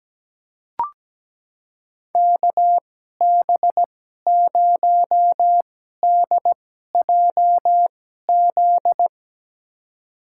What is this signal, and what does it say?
Morse code, heard as a keyed tone, about 17 wpm, spaced as for 12 wpm, 700 Hz. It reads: KB0DJZ
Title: USA Call Signs - 6 Characters - 2x Character Spacing - 17wpm